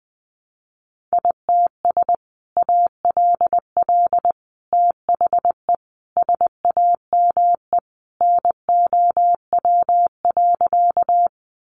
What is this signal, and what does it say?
Morse code, heard as a keyed tone, about 20 wpm, 700 Hz. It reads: ITS ALL THE SAME NOW.
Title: Sentences from Top 600 Words 20wpm